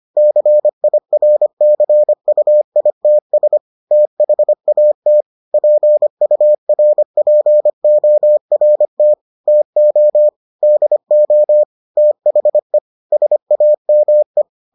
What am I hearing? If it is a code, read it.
Morse code, heard as a keyed tone, about 25 wpm, 600 Hz. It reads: CIRCUITS THAT PURPORT TO DO THE SAME